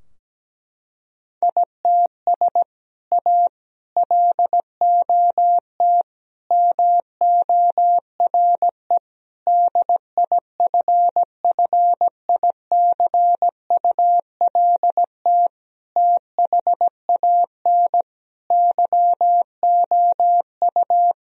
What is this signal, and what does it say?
Morse code, heard as a keyed tone, about 17 wpm, 700 Hz. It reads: ITS A LOT MORE DIFFICULT THAN YOU
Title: Sentences from Top 1000 Words - Review Part 3 - 17wpm